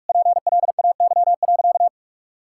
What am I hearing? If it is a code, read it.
Morse code, heard as a keyed tone, about 46 wpm, 700 Hz. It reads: PLAY.
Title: Sentences from Top 1000 Words - Review Part 6 - 45wpm